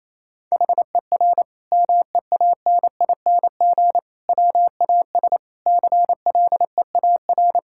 Morse code, heard as a keyed tone, about 28 wpm, 700 Hz. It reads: HER MEANING WAS CLEAR
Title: Sentences from Top 1000 Words - Review Part 6 - 28wpm